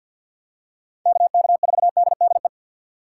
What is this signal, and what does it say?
Morse code, heard as a keyed tone, about 49 wpm, 700 Hz. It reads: KK4DDE
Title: USA Call Signs - 6 Characters - 3x Character Spacing - 50wpm